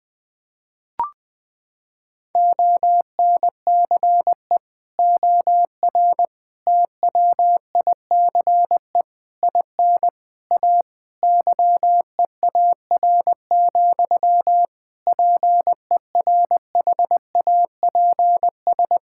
Morse code, heard as a keyed tone, about 20 wpm, 700 Hz. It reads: ONCE OR TWICE IN A YEAR, PERHAPS
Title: Sentences from Top 1600 Words 20wpm